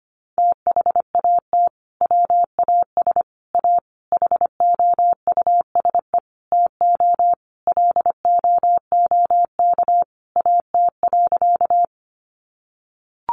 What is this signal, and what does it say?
Morse code, heard as a keyed tone, about 25 wpm, 700 Hz. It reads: THAT WAS A HOUSE TO LOOK AT.
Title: Sentences from Top 200 Words 25wpm